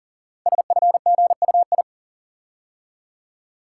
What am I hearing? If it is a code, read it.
Morse code, heard as a keyed tone, about 39 wpm, 690 Hz. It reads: SFGUI
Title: Code Groups - 5 Letter:Numbers - 40wpm